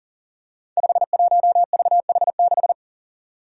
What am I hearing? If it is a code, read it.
Morse code, heard as a keyed tone, about 37 wpm, 700 Hz. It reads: 51VH6
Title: Five Challenging Letter-Number Combinations 40wpm